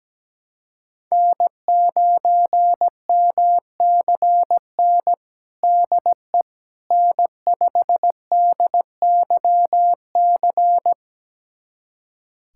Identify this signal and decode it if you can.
Morse code, heard as a keyed tone, about 17 wpm, 700 Hz. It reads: N9MCN DE N5DYC
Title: USA Callsign DE Callsign - 5 Characters - 17wpm